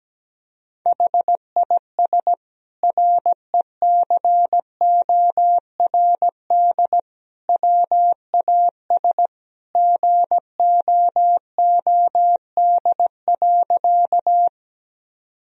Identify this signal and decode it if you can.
Morse code, heard as a keyed tone, about 17 wpm, 700 Hz. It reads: HIS RECORD WAS GOOD.
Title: Sentences from Top 1000 Words - Review Part 3 - 17wpm